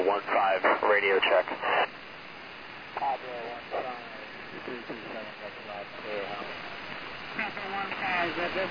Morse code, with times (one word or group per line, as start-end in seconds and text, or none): none